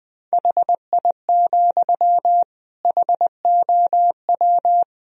Morse code, heard as a keyed tone, about 21 wpm, 700 Hz. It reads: HI, HOW
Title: Sentences from Top 1000 Words - Review Part 3 - 20wpm